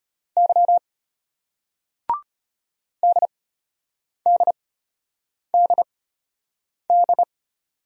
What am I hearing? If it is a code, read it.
Morse code, heard as a keyed tone, about 33 wpm, spaced as for 8 wpm, 700 Hz. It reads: Y DDDD